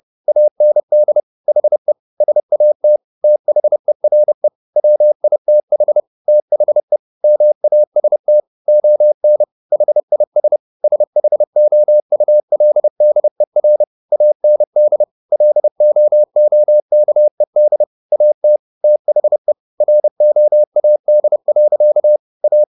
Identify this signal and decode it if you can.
Morse code, heard as a keyed tone, about 30 wpm, 600 Hz. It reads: AND HE SAT THERE WITH THE MAST ON HIS SHOULDER AND LOOKED AT THE ROAD. A